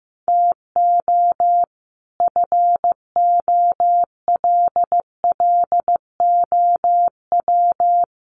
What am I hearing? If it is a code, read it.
Morse code, heard as a keyed tone, about 15 wpm, 700 Hz. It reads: TO FOLLOW
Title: Sentences from Top 1100 Words 15wpm